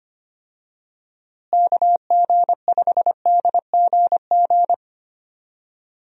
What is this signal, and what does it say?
Morse code, heard as a keyed tone, about 25 wpm, 700 Hz. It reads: KG5DGG
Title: USA Call Signs - 6 Characters - Repeated 3x - 25wpm